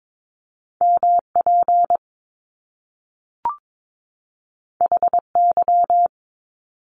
Morse code, heard as a keyed tone, about 22 wpm, 700 Hz. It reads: MP HY
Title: Top Two Letter Combinations in Words 22wpm